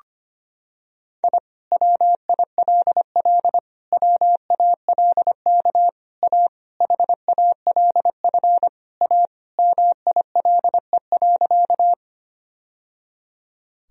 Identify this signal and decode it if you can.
Morse code, heard as a keyed tone, about 25 wpm, 700 Hz. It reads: I WILL WALK A HALF A MILE.
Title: Sentences from Top 1000 Words - Review Part 1 - 25wpm